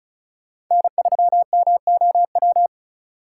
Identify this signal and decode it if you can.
Morse code, heard as a keyed tone, about 46 wpm, 700 Hz. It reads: N3MOW